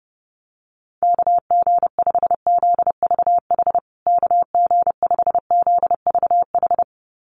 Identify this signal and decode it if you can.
Morse code, heard as a keyed tone, about 30 wpm, 700 Hz. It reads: KG5ZVH KG5ZVH
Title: USA Call Signs Repeated 2x 30wpm